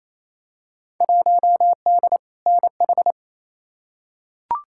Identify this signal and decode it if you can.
Morse code, heard as a keyed tone, about 28 wpm, 700 Hz. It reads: 1D NH
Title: Field Day Exchanges Encore - 28wpm